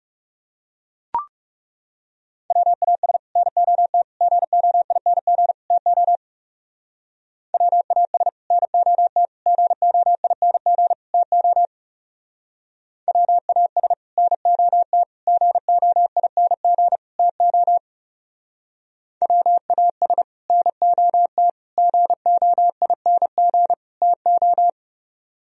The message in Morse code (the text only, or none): WAS NOT GOING TO WAS NOT GOING TO WAS NOT GOING TO WAS NOT GOING TO